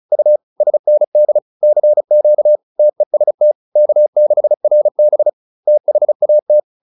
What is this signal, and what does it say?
Morse code, heard as a keyed tone, about 34 wpm, 600 Hz. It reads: U SND CQ TEST K6RB THAT